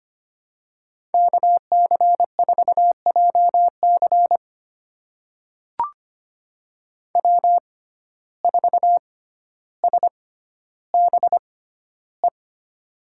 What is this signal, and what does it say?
Morse code, heard as a keyed tone, about 25 wpm, spaced as for 13 wpm, 700 Hz. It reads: KC4JC W4SBE
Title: USA Call Signs - 5 Characters - 6x Character Spacing - 25wpm